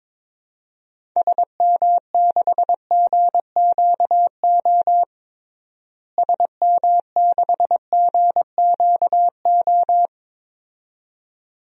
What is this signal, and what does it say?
Morse code, heard as a keyed tone, about 22 wpm, 700 Hz. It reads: SM6GQO SM6GQO
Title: DXCC-100 Call Signs - 6 Characters - Repeated 2x - 22wpm